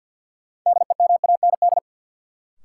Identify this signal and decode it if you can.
Morse code, heard as a keyed tone, about 52 wpm, 700 Hz. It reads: DEMAND
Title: Sentences from Top 800 Words - New Word Drill - 50wpm